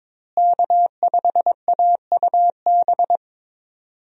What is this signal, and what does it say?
Morse code, heard as a keyed tone, about 22 wpm, 700 Hz. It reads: K5AUB
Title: USA Call Signs - 5 Characters - Repeated 2x - 22wpm